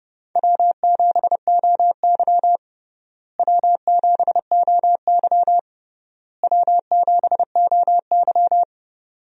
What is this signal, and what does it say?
Morse code, heard as a keyed tone, about 32 wpm, 700 Hz. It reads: W7OY W7OY W7OY
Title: USA Call Signs - 4 Characters - Repeated 3x - 30wpm